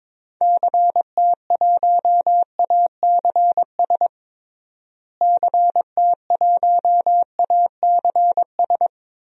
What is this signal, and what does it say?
Morse code, heard as a keyed tone, about 21 wpm, 700 Hz. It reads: CT1ACS CT1ACS
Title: DXCC-100 Call Signs - 6 Characters - Repeated 2x - 22wpm